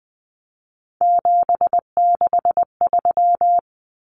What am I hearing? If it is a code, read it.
Morse code, heard as a keyed tone, about 20 wpm, 700 Hz. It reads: 763
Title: Numbers - 3-Digit 20wpm